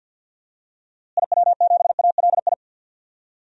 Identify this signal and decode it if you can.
Morse code, heard as a keyed tone, about 49 wpm, 700 Hz. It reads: IW7ALI
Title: DXCC-100 Call Signs - 6 Characters - Repeated 3x - 50wpm